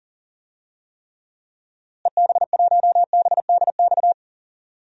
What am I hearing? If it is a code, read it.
Morse code, heard as a keyed tone, about 37 wpm, 700 Hz. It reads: EB1BDX
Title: DXCC-100 Call Signs - 6 Characters - Repeated 2x - 40wpm